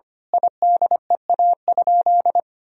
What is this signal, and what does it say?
Morse code, heard as a keyed tone, about 25 wpm, 700 Hz. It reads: IDEA?
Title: Sentences from Top 1000 Words - Review Part 1 - 25wpm